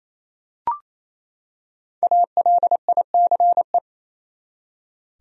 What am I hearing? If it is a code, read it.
Morse code, heard as a keyed tone, about 28 wpm, 700 Hz. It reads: ALICE
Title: USA Common Names - Max 5 Letters - 28wpm